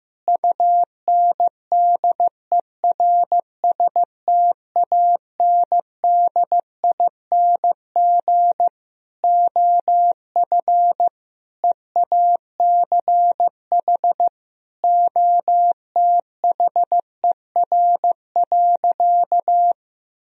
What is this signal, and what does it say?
Morse code, heard as a keyed tone, about 15 wpm, 700 Hz. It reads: UNDERSTANDING OF EACH OTHER.
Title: Sentences from Top 1000 Words - Review Part 3 - 15wpm